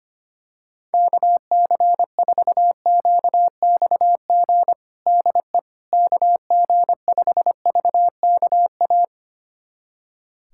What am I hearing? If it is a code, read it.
Morse code, heard as a keyed tone, about 25 wpm, 700 Hz. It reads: KC4QXG DE KG5VKA
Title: USA Callsign DE Callsign - 6 Characters - 25wpm